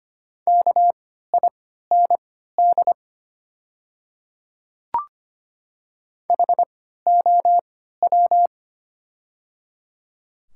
Morse code, heard as a keyed tone, about 25 wpm, spaced as for 14 wpm, 700 Hz. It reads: KIND HOW
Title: 3x Character Spacing - Top 200 Words - 25wpm